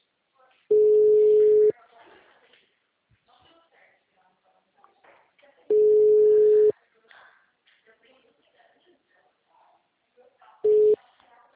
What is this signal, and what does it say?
Ringback tone (Europe, 425 Hz), with speech.
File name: out- -2006- - - .564